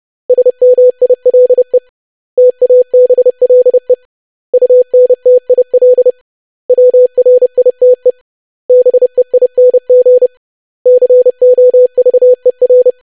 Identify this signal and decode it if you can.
Morse code, heard as a keyed tone, about 30 wpm, 500 Hz. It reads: SMILE TABLE UNTIL WRITE BEING COVER
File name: lesson / web / basic500-25wpm